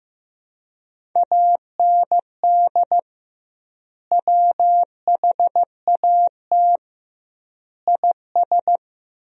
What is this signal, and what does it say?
Morse code, heard as a keyed tone, about 16 wpm, 700 Hz. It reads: AND WHAT IS
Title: 2x Word-Spacing - Sentences from Top 100 Words - 15wpm